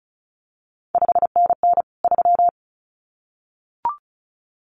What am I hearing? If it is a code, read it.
Morse code, heard as a keyed tone, about 36 wpm, 700 Hz. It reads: 5NN 3
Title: CQ WW DX Contest Exchange Encore - 35wpm